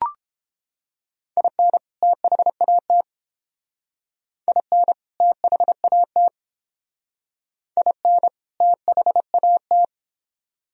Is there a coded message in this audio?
Yes: Morse code from 1.4 to 9.8 s, heard as a keyed tone, about 30 wpm, 700 Hz. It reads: IN THAT IN THAT IN THAT